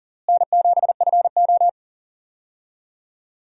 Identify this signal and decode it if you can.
Morse code, heard as a keyed tone, about 40 wpm, 700 Hz. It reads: N7FO
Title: USA Call Signs - 4 Characters - Repeated 2x - 40wpm